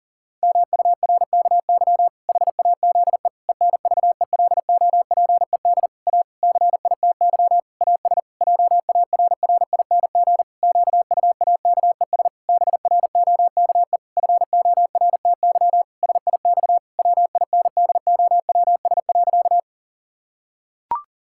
Morse code, heard as a keyed tone, about 40 wpm, 700 Hz. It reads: MURKY HAZE ENVELOPED A CITY AS JARRING QUAKES BROKE FORTY SIX WINDOWS.
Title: Pangram Challenge v2 - 40wpm